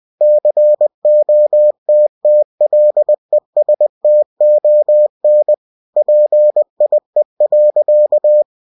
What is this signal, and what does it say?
Morse code, heard as a keyed tone, about 20 wpm, 600 Hz. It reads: COTTLESTON PIE.
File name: cw-books / winnie-the-pooh / winnie-20wpm-page-0046